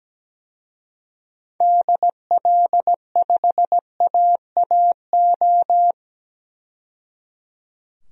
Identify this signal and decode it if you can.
Morse code, heard as a keyed tone, about 17 wpm, 700 Hz. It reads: DL5AAO
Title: DXCC-100 Call Signs - 6 Characters - 17wpm